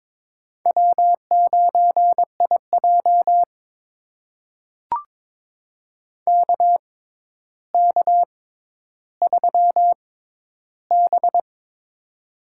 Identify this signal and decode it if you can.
Morse code, heard as a keyed tone, about 22 wpm, spaced as for 11 wpm, 700 Hz. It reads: W9IJ KK3B
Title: USA Call Signs - 4 Characters - 6x Character Spacing - 22wpm